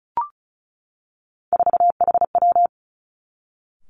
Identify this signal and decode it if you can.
Morse code, heard as a keyed tone, about 36 wpm, 700 Hz. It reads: 4HW
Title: Three Challenging Letter-Number Combinations 35wpm